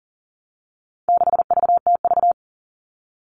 Morse code, heard as a keyed tone, about 40 wpm, 700 Hz. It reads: 6VTV
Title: All Cut Numbers - 4-Digit 40wpm